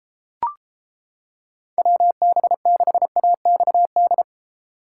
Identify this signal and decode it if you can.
Morse code, heard as a keyed tone, about 32 wpm, 700 Hz. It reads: WB6AXD